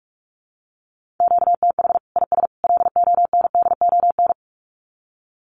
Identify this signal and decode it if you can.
Morse code, heard as a keyed tone, about 44 wpm, 700 Hz. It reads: QTH IS LONDON
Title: Common QSO Phrases 45wpm